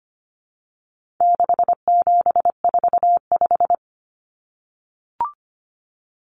Morse code, heard as a keyed tone, about 25 wpm, 700 Hz. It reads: 6745
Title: Common Cut Numbers - 4-Digit 25wpm